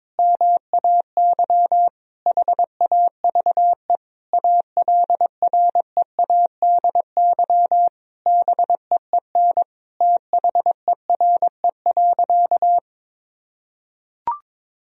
Morse code, heard as a keyed tone, about 22 wpm, 700 Hz. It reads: MAY HAVE ALREADY BEEN THERE.
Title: Sentences from Top 1000 Words - Review Part 3 - 22wpm